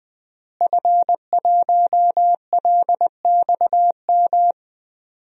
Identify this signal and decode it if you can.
Morse code, heard as a keyed tone, about 21 wpm, 700 Hz. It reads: F1LXM
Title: DXCC-100 Call Signs - 5 Characters - Repeated 2x - 20wpm